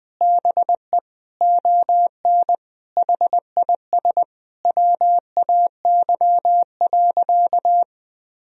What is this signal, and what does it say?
Morse code, heard as a keyed tone, about 20 wpm, 700 Hz. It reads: BE ON HIS WAY.
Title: Sentences from Top 300 Words and Abbreviations 20wpm